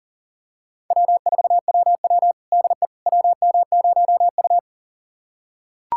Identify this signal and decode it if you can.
Morse code, heard as a keyed tone, about 40 wpm, 700 Hz. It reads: W4WW DE WM0U
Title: USA Callsign DE Callsign - 4 Characters - 40wpm